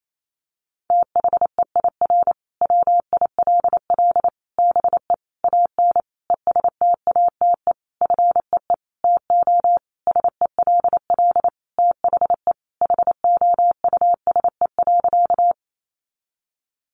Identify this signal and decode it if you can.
Morse code, heard as a keyed tone, about 28 wpm, 700 Hz. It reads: THEIR WILL BE AN ESTATE FEE TO SELL THE HOUSE.
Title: Sentences from Top 1500 Words 28wpm